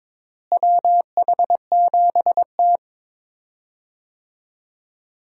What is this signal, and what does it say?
Morse code, heard as a keyed tone, about 22 wpm, 700 Hz. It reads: WH7T
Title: USA Call Signs - 4 Characters - Repeated 2x - 22wpm